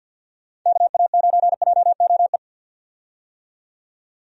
Morse code, heard as a keyed tone, about 48 wpm, 700 Hz. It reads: KA9JOE
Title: USA Call Signs - 6 Characters - 50wpm